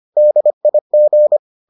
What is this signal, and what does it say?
Morse code, heard as a keyed tone, about 25 wpm, 600 Hz. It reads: DIG